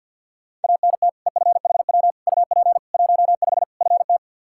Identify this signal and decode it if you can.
Morse code, heard as a keyed tone, about 50 wpm, 700 Hz. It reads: ANT EFHW UP 15 FT
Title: Practice QSO - Exchange 2 - Info - 50wpm